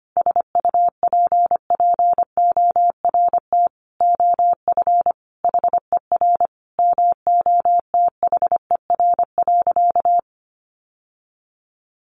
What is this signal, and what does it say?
Morse code, heard as a keyed tone, about 25 wpm, 700 Hz. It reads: SUPPORT OF HER MOTHER.
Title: Sentences from Top 800 Words 25wpm